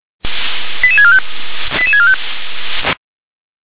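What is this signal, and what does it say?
Five-tone selective call (ZVEI), heard as quick runs of five tones, tones of about 70 ms. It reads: 88645 88645
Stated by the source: ZVEI five-tone selective call received on an SDR.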